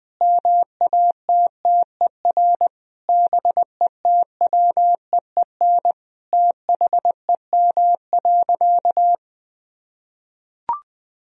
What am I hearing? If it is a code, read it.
Morse code, heard as a keyed tone, about 20 wpm, 700 Hz. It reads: MATTER BETWEEN THEM.
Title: Sentences from Top 1000 Words - Review Part 4 - 20wpm